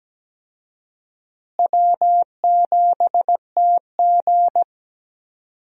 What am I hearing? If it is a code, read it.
Morse code, heard as a keyed tone, about 17 wpm, 700 Hz. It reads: W7TG
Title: USA Call Signs - 4 Characters - Repeated 2x - 17wpm